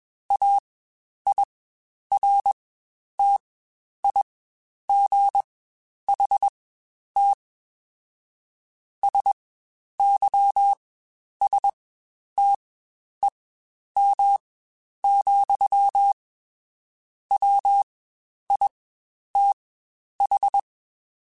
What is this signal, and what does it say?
Morse code, heard as a keyed tone, about 21 wpm, spaced as for 10 wpm, 800 Hz. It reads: AIRTIGHT SYSTEM, WITH
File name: Quotes Spaced / Quotes Spaced-mp3s / Airtight sp @20